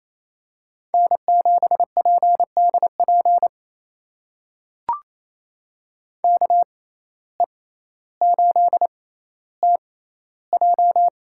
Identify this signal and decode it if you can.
Morse code, heard as a keyed tone, about 28 wpm, spaced as for 14 wpm, 700 Hz. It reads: N7PDP KE8TJ
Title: USA Call Signs - 5 Characters - 6x Character Spacing - 28wpm